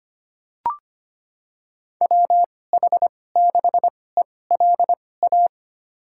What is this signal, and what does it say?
Morse code, heard as a keyed tone, about 25 wpm, spaced as for 18 wpm, 700 Hz. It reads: WH6ELA
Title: USA Call Signs - 6 Characters - 2x Character Spacing - 25wpm